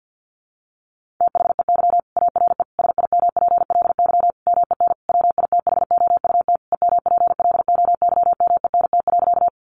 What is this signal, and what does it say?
Morse code, heard as a keyed tone, about 50 wpm, 700 Hz. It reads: THEY ARE SIMPLY MEN WITHOUT EMPLOYMENT.
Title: Sentences from Top 1500 Words 50wpm